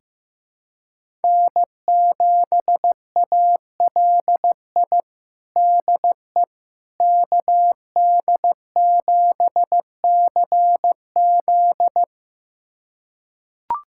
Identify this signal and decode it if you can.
Morse code, heard as a keyed tone, about 16 wpm, 700 Hz. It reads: N7ALI DE KD7CZ
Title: USA Callsign DE Callsign - 5 Characters - 15wpm